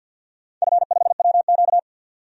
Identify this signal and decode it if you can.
Morse code, heard as a keyed tone, about 52 wpm, 700 Hz. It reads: F5WQ
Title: DXCC-100 Call Signs - 4 Characters - 50wpm